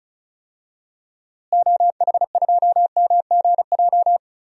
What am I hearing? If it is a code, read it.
Morse code, heard as a keyed tone, about 36 wpm, 700 Hz. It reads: OH2MGJ
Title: DXCC-100 Call Signs - 6 Characters - Repeated 3x - 35wpm